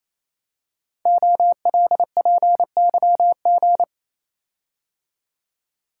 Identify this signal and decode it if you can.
Morse code, heard as a keyed tone, about 28 wpm, 700 Hz. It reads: OLPYG
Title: Code Groups - 5 Letters - 28wpm